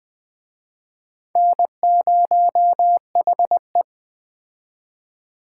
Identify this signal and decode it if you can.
Morse code, heard as a keyed tone, about 20 wpm, 700 Hz. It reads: N0HE
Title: Speed Racing - USA Call Signs 17wpm